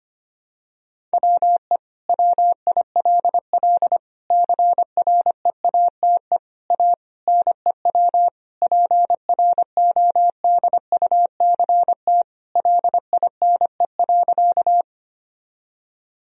Morse code, heard as a keyed tone, about 25 wpm, 700 Hz. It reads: WE WILL CREATE A NEW PRODUCT LINE.
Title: Sentences from Top 1000 Words - Review Part 3 - 25wpm